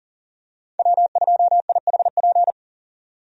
Morse code, heard as a keyed tone, about 40 wpm, 700 Hz. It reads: W2IHP
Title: USA Call Signs - 5 Characters - 40wpm